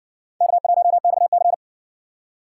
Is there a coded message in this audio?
Yes: Morse code from 0.4 to 1.5 s, heard as a keyed tone, about 56 wpm, 700 Hz. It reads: K1XC